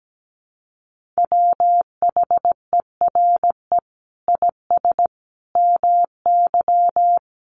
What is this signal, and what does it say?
Morse code, heard as a keyed tone, about 17 wpm, 700 Hz. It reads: WHERE IS MY
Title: Sentences from Top 1400 Words 17wpm